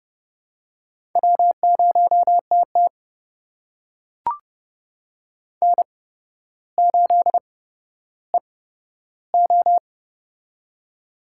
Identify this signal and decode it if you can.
Morse code, heard as a keyed tone, about 31 wpm, spaced as for 13 wpm, 700 Hz. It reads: W0TT N8EO
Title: USA Call Signs - 4 Characters - 8x Character Spacing - 30wpm